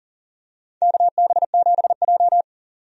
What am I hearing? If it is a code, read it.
Morse code, heard as a keyed tone, about 40 wpm, 700 Hz. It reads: KB7J
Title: USA Call Signs - 4 Characters - 40wpm